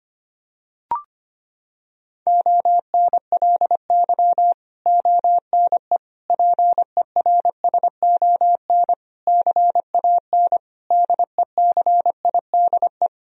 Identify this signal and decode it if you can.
Morse code, heard as a keyed tone, about 25 wpm, 700 Hz. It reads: ONLY ONE PERSON CAN DECIDE
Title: Sentences from Top 1000 Words - Review Part 3 - 25wpm